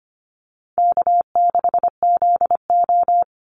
Morse code, heard as a keyed tone, about 25 wpm, 700 Hz. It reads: K6ZO
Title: Call Signs - 25wpm